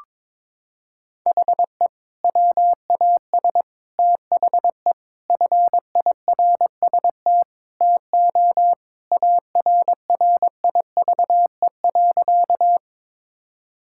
Morse code, heard as a keyed tone, about 22 wpm, 700 Hz. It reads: HE WAS THE FIRST TO ARRIVE.
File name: practice / 22wpm / Sentences with 2-Word Phrases - 22wpm